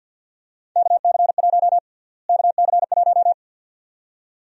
Morse code, heard as a keyed tone, about 46 wpm, 700 Hz. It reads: KC1 KC1
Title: USA Call Sign Prefixes - Repeated 2x - 50wpm